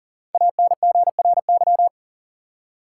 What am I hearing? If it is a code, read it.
Morse code, heard as a keyed tone, about 40 wpm, 700 Hz. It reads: ANGRY
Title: Five Letter Words - 40wpm